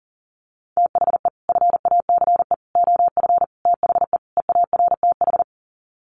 Morse code, heard as a keyed tone, about 40 wpm, 700 Hz. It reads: THE FACE OF THE EARTH
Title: Sets of 5 Words Encore 40wpm